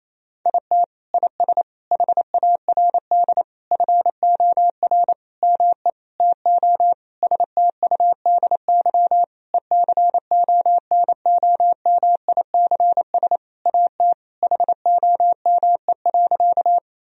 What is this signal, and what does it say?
Morse code, heard as a keyed tone, about 28 wpm, 700 Hz. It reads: IT IS HARD FOR ME TO STUDY ECONOMICS AT HOME.
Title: Sentences from Top 1000 Words - Review Part 3 - 28wpm